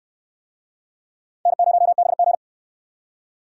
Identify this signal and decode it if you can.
Morse code, heard as a keyed tone, about 65 wpm, 700 Hz. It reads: N0BG